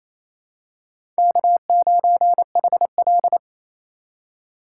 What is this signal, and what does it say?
Morse code, heard as a keyed tone, about 28 wpm, 700 Hz. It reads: K9HL
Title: USA Call Signs - 2x Character Spacing - 28wpm